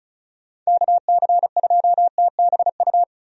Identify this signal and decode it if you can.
Morse code, heard as a keyed tone, about 34 wpm, 700 Hz. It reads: KC2TBU
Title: USA Call Signs - 6 Characters - 35wpm